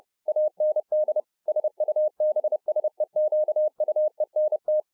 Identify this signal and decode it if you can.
Morse code, heard as a keyed tone, about 30 wpm, 610 Hz. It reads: AND SUBSEQUENT